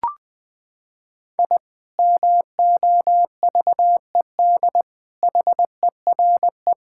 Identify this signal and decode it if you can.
Morse code, heard as a keyed tone, about 20 wpm, 700 Hz. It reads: I MOVED HERE
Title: Sentences with 7-Word Phrases - 20wpm